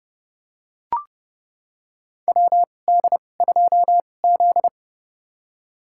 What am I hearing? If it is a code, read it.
Morse code, heard as a keyed tone, about 30 wpm, spaced as for 22 wpm, 700 Hz. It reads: WD2Z
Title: USA Call Signs - 2x Character Spacing - 30wpm